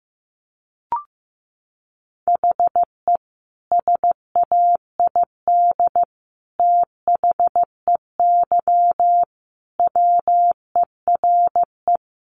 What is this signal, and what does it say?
Morse code, heard as a keyed tone, about 15 wpm, 700 Hz. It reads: HE SAID THEY WERE
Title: Sentences from Top 700 Words 15wpm